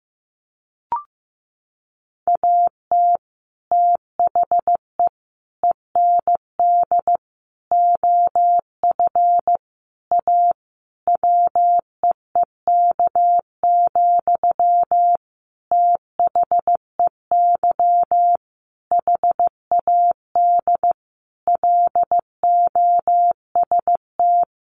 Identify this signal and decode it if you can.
Morse code, heard as a keyed tone, about 15 wpm, 700 Hz. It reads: AT THE END OF A WEEK, THEY HAD LOST
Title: Sentences from Top 1400 Words 15wpm